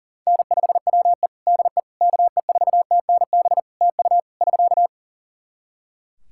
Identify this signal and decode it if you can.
Morse code, heard as a keyed tone, about 40 wpm, 700 Hz. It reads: N5WE DE KE4TNB TU <SK>
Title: K1USN SST QSO Practice 40wpm